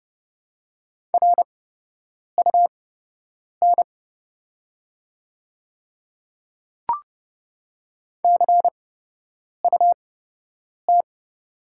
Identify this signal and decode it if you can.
Morse code, heard as a keyed tone, about 30 wpm, spaced as for 8 wpm, 700 Hz. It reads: RUN CUT
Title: 8x Character Spacing - Top 500 Words - 30wpm